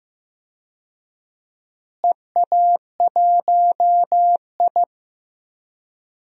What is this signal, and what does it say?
Morse code, heard as a keyed tone, about 15 wpm, 700 Hz. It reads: EA1I
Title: DXCC-100 Call Signs - 4 Characters - Repeated 3x - 15wpm